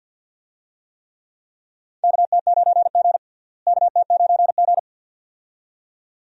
Morse code, heard as a keyed tone, about 52 wpm, 700 Hz. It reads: KT9G KT9G
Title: USA Call Signs - 4 Characters - Repeated 2x - 50wpm